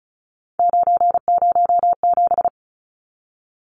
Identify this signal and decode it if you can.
Morse code, heard as a keyed tone, about 34 wpm, 700 Hz. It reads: 907
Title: Numbers - 3-Digit 35wpm